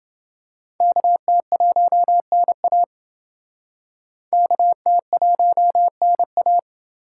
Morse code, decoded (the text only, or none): KT1NA KT1NA